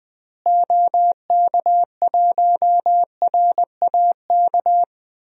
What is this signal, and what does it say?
Morse code, heard as a keyed tone, about 20 wpm, 700 Hz. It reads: OK1RAK
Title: DXCC-100 Call Signs - 6 Characters - Repeated 3x - 20wpm